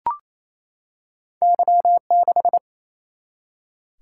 Morse code, heard as a keyed tone, about 28 wpm, 700 Hz. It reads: Y6
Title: Single-Letter Single-Number 28wpm